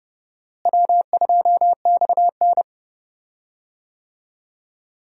Morse code, heard as a keyed tone, about 29 wpm, 700 Hz. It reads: W2XN
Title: USA Call Signs - 4 Characters - Repeated 2x - 30wpm